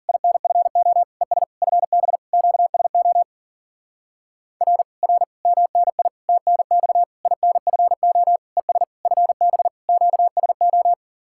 Morse code, heard as a keyed tone, about 52 wpm, 700 Hz. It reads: INFO ES FB QSO R R MNI TNX INFO ES FB QSO